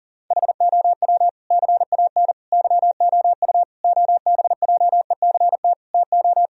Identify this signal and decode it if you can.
Morse code, heard as a keyed tone, about 40 wpm, 700 Hz. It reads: HOW CAN YOU OBJECT TO
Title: Sentences from Top 1000 Words - Review Part 3 - 40wpm